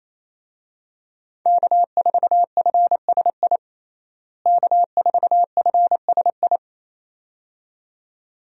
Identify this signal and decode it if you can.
Morse code, heard as a keyed tone, about 28 wpm, 700 Hz. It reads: K4FSI K4FSI
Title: USA Call Signs - 5 Characters - Repeated 2x - 28wpm